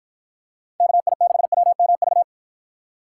Morse code, heard as a keyed tone, about 50 wpm, 700 Hz. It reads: KI6WMV